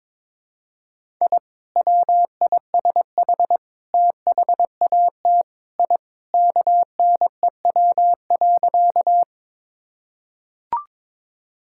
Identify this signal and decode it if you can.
Morse code, heard as a keyed tone, about 22 wpm, 700 Hz. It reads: I WISH THAT I KNEW.
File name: practice / 22wpm / Sentences with 2-Word Phrases - 22wpm